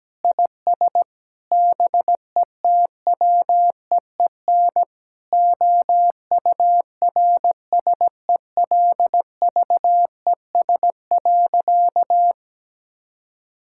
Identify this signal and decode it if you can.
Morse code, heard as a keyed tone, about 17 wpm, 700 Hz. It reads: IS BETWEEN OURSELVES.
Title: Sentences from Top 1600 Words 17wpm